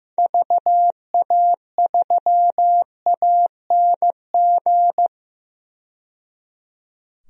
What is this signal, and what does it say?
Morse code, heard as a keyed tone, about 15 wpm, 700 Hz. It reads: VA3ANG
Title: DXCC-100 Call Signs - 6 Characters - 15wpm